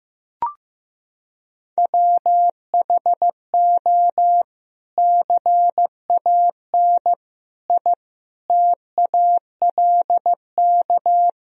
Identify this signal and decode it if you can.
Morse code, heard as a keyed tone, about 15 wpm, 700 Hz. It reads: WHO CAN I TALK
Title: Sentences from Top 1000 Words - Review Part 4 - 15wpm